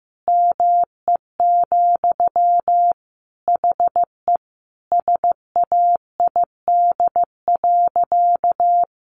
Morse code, heard as a keyed tone, about 14 wpm, 700 Hz. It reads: ME, HE SAID.
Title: Sentences from Top 1000 Words 15wpm